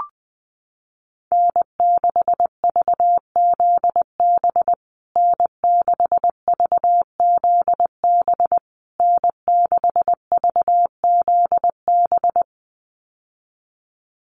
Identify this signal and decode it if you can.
Morse code, heard as a keyed tone, about 20 wpm, 700 Hz. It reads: N6VZB N6VZB N6VZB
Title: USA Call Signs Repeated 3x 20wpm